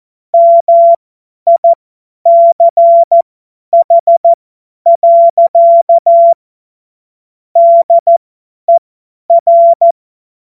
Morse code, heard as a keyed tone, about 14 wpm, spaced as for 10 wpm, 680 Hz. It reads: MICH. DER
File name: Morsekurs / Witze / Witz 02 April 2024 50 F70